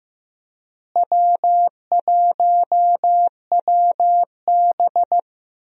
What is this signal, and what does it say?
Morse code, heard as a keyed tone, about 16 wpm, 700 Hz. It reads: W1WB
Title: USA Call Signs - 4 Characters - Repeated 3x - 15wpm